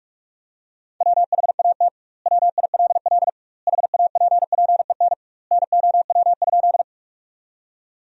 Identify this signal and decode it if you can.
Morse code, heard as a keyed tone, about 46 wpm, 700 Hz. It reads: WHAT WILL HAPPEN NOW?
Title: 2x Word-Spacing - Sentences from Top 400 Words - 45wpm